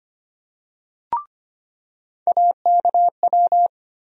Morse code, heard as a keyed tone, about 25 wpm, 700 Hz. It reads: AKW
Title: USA Call Sign Postfixes - 3 Letters 25wpm